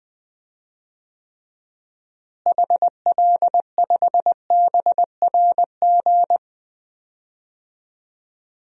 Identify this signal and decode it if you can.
Morse code, heard as a keyed tone, about 20 wpm, 700 Hz. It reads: HL5BRG
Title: DXCC-100 Call Signs - 6 Characters - Repeated 3x - 20wpm